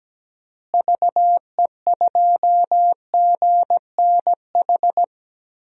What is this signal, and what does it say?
Morse code, heard as a keyed tone, about 17 wpm, 700 Hz. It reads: VE2GNH
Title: DXCC-100 Call Signs - 6 Characters - Repeated 3x - 17wpm